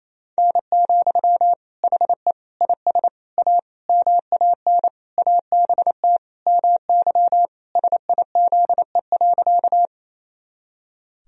Morse code, heard as a keyed tone, about 27 wpm, 700 Hz. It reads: N, HE IS A MAN ABT MY SIZE.
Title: Sentences from Top 400 Words and Abbreviations 28wpm